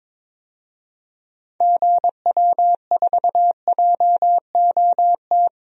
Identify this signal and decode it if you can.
Morse code, heard as a keyed tone, about 22 wpm, 700 Hz. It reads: GW4JOT
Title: DXCC-100 Call Signs - 6 Characters - 22wpm